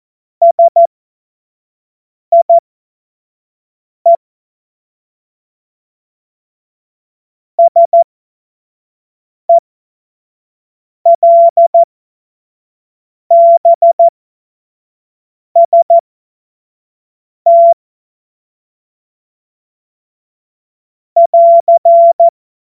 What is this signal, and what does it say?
Morse code, heard as a keyed tone, about 14 wpm, spaced as for 5 wpm, 680 Hz. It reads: SIE SELBST <AR>
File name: Morsekurs / Witze / Witz 25. Januar 2022, 25 F70